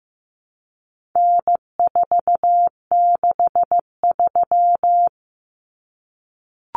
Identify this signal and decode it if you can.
Morse code, heard as a keyed tone, about 15 wpm, 700 Hz. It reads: N463
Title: Common Cut Numbers - 4-Digit 15wpm